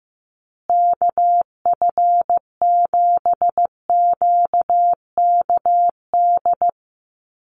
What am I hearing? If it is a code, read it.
Morse code, heard as a keyed tone, about 15 wpm, 700 Hz. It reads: KF7QKD
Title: USA Call Signs Repeated 2x 15wpm